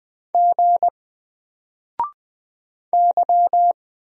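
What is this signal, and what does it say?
Morse code, heard as a keyed tone, about 20 wpm, 700 Hz. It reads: G Y
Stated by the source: Single letter and number Morse drill at 20 wpm, from N4LCD.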